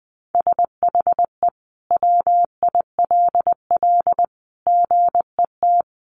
Morse code, heard as a keyed tone, about 20 wpm, 700 Hz. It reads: SHE WILL GET
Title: Sentences from Top 1200 Words 20wpm